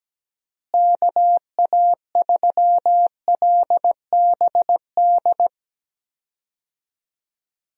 Morse code, heard as a keyed tone, about 17 wpm, 700 Hz. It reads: KA3LBD
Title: USA Call Signs - 6 Characters - Repeated 2x - 17wpm